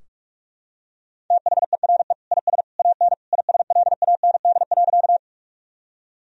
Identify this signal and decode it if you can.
Morse code, heard as a keyed tone, about 41 wpm, 700 Hz. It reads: THERE IS AN ISLAND.
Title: Sentences from Top 1000 Words - Review Part 3 - 45wpm